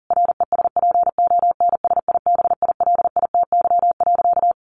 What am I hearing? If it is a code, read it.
Morse code, heard as a keyed tone, about 40 wpm, 700 Hz. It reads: RESPONSIBILITY.
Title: Sentences from Top 1200 Words 40wpm